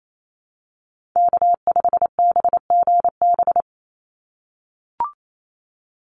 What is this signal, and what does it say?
Morse code, heard as a keyed tone, about 28 wpm, 700 Hz. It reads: K5BGB
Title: Call Signs - 28wpm